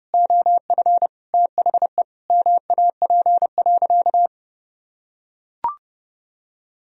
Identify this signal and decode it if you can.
Morse code, heard as a keyed tone, about 32 wpm, 700 Hz. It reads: OF THE MAP.
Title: Sentences from Top 1000 Words - Review Part 2 - 30wpm